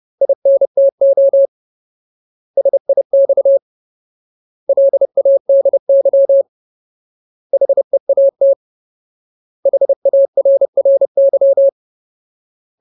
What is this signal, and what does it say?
Morse code, heard as a keyed tone, about 30 wpm, 550 Hz. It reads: INTO SIX LADY HEAT HARRY
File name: Ultimate Word List / 30WPM Ultimate